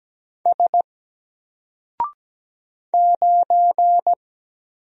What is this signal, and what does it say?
Morse code, heard as a keyed tone, about 17 wpm, 700 Hz. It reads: S 9
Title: Single Character (Letter:Number:Punctuation:Prosign) 17wpm